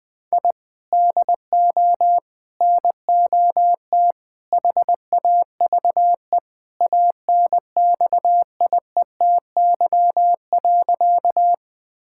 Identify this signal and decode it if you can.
Morse code, heard as a keyed tone, about 20 wpm, 700 Hz. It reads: I DO NOT HAVE ANXIETY.
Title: Sentences from Top 1600 Words 20wpm